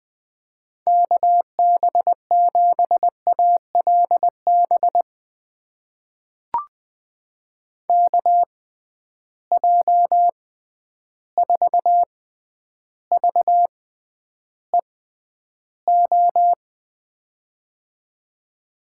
Morse code, heard as a keyed tone, about 21 wpm, spaced as for 10 wpm, 700 Hz. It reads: KB7ALB KJ4VEO
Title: USA Call Signs - 6 Characters - 6x Character Spacing - 20wpm